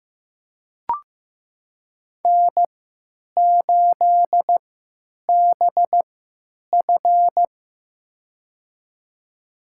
Morse code, heard as a keyed tone, about 15 wpm, spaced as for 9 wpm, 700 Hz. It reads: N8BF
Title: USA Call Signs - 4 Characters - 3x Character Spacing - 15wpm